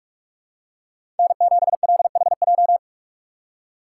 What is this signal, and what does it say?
Morse code, heard as a keyed tone, about 46 wpm, 700 Hz. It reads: N7LHJ
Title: USA Call Signs - 5 Characters - 2x Character Spacing - 45wpm